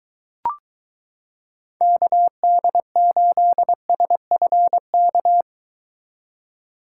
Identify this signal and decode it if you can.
Morse code, heard as a keyed tone, about 22 wpm, 700 Hz. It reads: KD8SFK